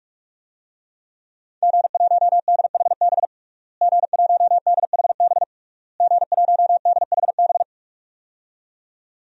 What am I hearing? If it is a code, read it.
Morse code, heard as a keyed tone, about 46 wpm, 700 Hz. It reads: G1DHB G1DHB G1DHB
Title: DXCC-100 Call Signs - 5 Characters - Repeated 3x - 45wpm